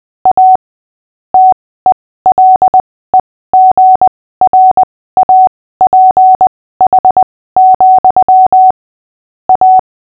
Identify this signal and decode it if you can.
Morse code, heard as a keyed tone, about 20 wpm, spaced as for 15 wpm, 750 Hz. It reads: A TELEGRAPH, A
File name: cw / fists / Short-Story-6-15wpm